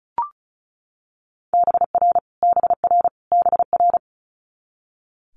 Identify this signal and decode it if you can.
Morse code, heard as a keyed tone, about 36 wpm, 700 Hz. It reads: BR BR BR
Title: Multi-letter Phonemes Repeated 3x 35wpm